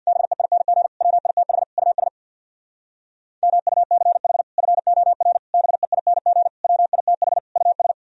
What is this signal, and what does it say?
Morse code, heard as a keyed tone, about 65 wpm, 700 Hz. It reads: BEING WITH US MUCH FOR BEING WITH US